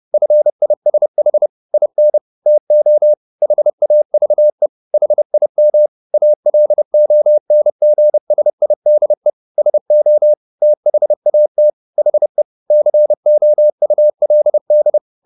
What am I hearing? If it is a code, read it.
Morse code, heard as a keyed tone, about 30 wpm, 600 Hz. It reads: FISH IN TO HAVE HIM ALONGSIDE SO THAT HE COULD